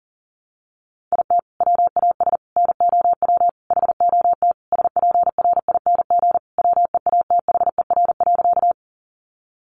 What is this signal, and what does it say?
Morse code, heard as a keyed tone, about 40 wpm, 700 Hz. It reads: IT WAS NOW HOT SPRING WEATHER.
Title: Sentences from Top 1200 Words 40wpm